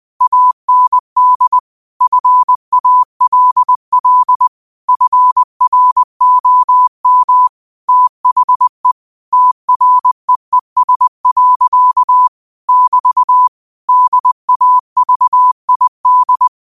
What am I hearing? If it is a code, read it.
Morse code, heard as a keyed tone, about 20 wpm, 1000 Hz. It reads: AND FALL FROM THE TREES. <BT> DAVID
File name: qotdfiles / 20wpm